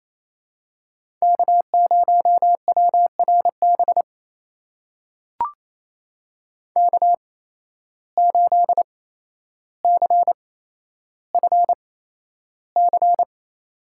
Morse code, heard as a keyed tone, about 28 wpm, spaced as for 12 wpm, 700 Hz. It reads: K0WRB K8CFC
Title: USA Call Signs - 5 Characters - 8x Character Spacing - 28wpm